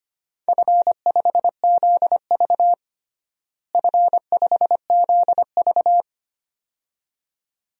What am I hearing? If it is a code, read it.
Morse code, heard as a keyed tone, about 25 wpm, 700 Hz. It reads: F5ZV F5ZV
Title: DXCC-100 Call Signs - 4 Characters - Repeated 2x - 25wpm